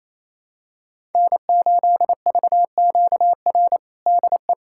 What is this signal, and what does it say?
Morse code, heard as a keyed tone, about 27 wpm, 690 Hz. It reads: N8VQR DE
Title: USA Callsign DE Callsign - 5 Characters - 28wpm